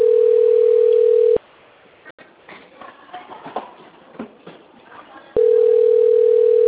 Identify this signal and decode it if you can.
Ringback tone (North America), 440 + 480 Hz.